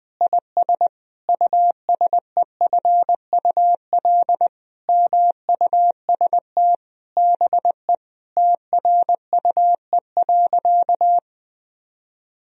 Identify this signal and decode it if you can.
Morse code, heard as a keyed tone, about 20 wpm, 700 Hz. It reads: IS USEFUL MUST BE TRUE.
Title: Sentences from Top 1000 Words - Review Part 4 - 20wpm